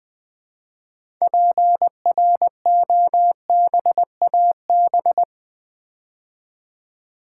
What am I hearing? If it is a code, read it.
Morse code, heard as a keyed tone, about 20 wpm, 700 Hz. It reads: PROBAB
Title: Build A Word - 2 Letters At A Time - 8-Letter Words - 20wpm